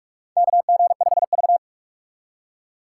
Morse code, heard as a keyed tone, about 46 wpm, 700 Hz. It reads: KG5V
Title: K1USN SST QSO Practice - 45wpm